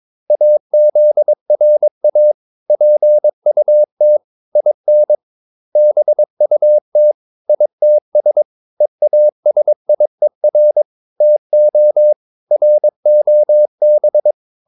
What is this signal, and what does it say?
Morse code, heard as a keyed tone, about 22 wpm, 600 Hz. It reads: AZRA PUT IN BUT ITS EASIER TO ROB
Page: Romanen Never af Ken Follet på engelsk.